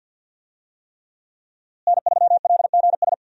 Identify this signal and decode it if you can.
Morse code, heard as a keyed tone, about 53 wpm, 700 Hz. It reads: N3LGS
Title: USA Call Signs - 5 Characters - Repeated 3x - 50wpm